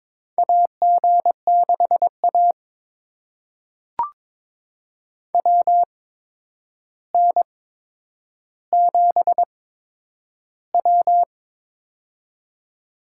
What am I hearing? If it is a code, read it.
Morse code, heard as a keyed tone, about 22 wpm, spaced as for 9 wpm, 700 Hz. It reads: AG6A WN7W
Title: USA Call Signs - 4 Characters - 8x Character Spacing - 22wpm